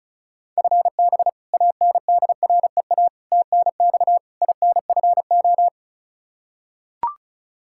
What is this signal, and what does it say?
Morse code, heard as a keyed tone, about 36 wpm, 700 Hz. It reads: FB ANDREA TNX INFO
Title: Practice QSO - Exchange 3 - More Info - 35wpm